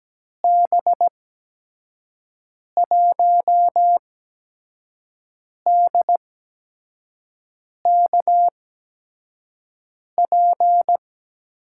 Morse code, heard as a keyed tone, about 17 wpm, spaced as for 5 wpm, 700 Hz. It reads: B1DKP
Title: USA Call Signs - 6 Characters - 8x Character Spacing - 17wpm